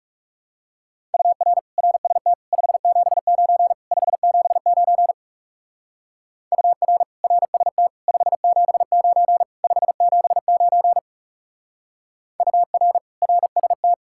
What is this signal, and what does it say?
Morse code, heard as a keyed tone, about 35 wpm, 700 Hz. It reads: UR RST 579 579 UR RST 579 579 UR RST
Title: Speed-Racing Common QSO Phrases - 30wpm